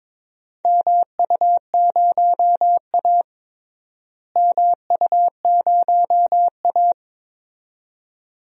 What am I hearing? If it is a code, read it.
Morse code, heard as a keyed tone, about 22 wpm, 700 Hz. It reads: MU0A MU0A
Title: DXCC-100 Call Signs - 4 Characters - Repeated 2x - 22wpm